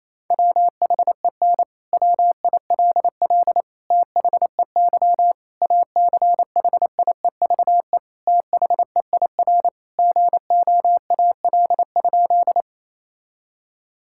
Morse code, heard as a keyed tone, about 27 wpm, 700 Hz. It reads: WHEN WILL THEY ACHIEVE THEIR GOAL?
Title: Sentences from Top 1000 Words - Review Part 6 - 28wpm